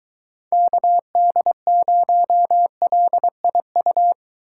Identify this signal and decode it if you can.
Morse code, heard as a keyed tone, about 23 wpm, 700 Hz. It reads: KD0LIU